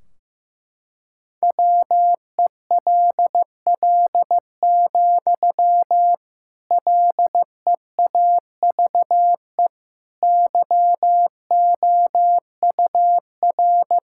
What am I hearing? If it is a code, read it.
Morse code, heard as a keyed tone, about 15 wpm, 700 Hz. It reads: WELL, LEAVE YOUR
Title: Sentences from Top 1600 Words 15wpm